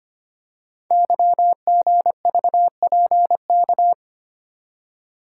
Morse code, heard as a keyed tone, about 25 wpm, 700 Hz. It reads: YGVPK
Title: Code Groups - 5 Letters - 25wpm